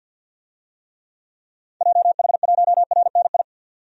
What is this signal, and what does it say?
Morse code, heard as a keyed tone, about 50 wpm, 700 Hz. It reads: JH1RNI
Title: DXCC-100 Call Signs - 6 Characters - 50wpm